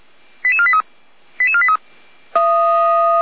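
Five-tone selective call (ZVEI), heard as quick runs of five tones, tones of about 70 ms. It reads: 88483 88483
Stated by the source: Received on an SDR; ZVEI five-tone selective call.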